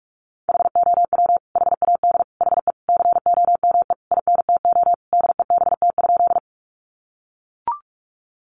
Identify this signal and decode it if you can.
Morse code, heard as a keyed tone, about 44 wpm, 700 Hz. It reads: HOW HAD HE COME INTO DEBT?
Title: Sentences from Top 1300 Words 45wpm